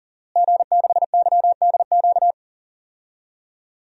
Morse code, heard as a keyed tone, about 37 wpm, 700 Hz. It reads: G6YDQ
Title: DXCC-100 Call Signs - 5 Characters - Repeated 2x - 40wpm